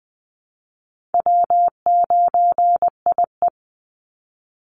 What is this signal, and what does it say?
Morse code, heard as a keyed tone, about 20 wpm, 700 Hz. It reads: W9IE
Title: Easy Call Signs Repeated 3x - 20wpm